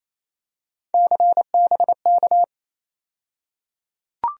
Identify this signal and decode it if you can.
Morse code, heard as a keyed tone, about 28 wpm, 700 Hz. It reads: CBK
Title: USA Call Sign Postfixes - 3 Letters 28wpm